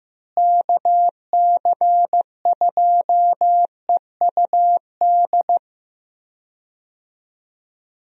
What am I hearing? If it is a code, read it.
Morse code, heard as a keyed tone, about 16 wpm, 700 Hz. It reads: KC2EUD
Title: USA Call Signs - 6 Characters - Repeated 2x - 15wpm